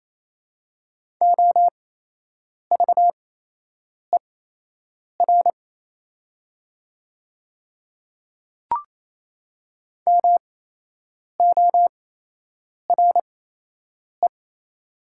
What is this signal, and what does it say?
Morse code, heard as a keyed tone, about 28 wpm, spaced as for 8 wpm, 700 Hz. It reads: OVER MORE
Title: 8x Character Spacing - Top 100 Words - 28wpm